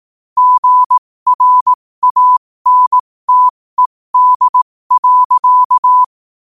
Morse code, heard as a keyed tone, about 18 wpm, spaced as for 15 wpm, 1000 Hz. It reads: GRANTED.
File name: qotdfiles / 15wpm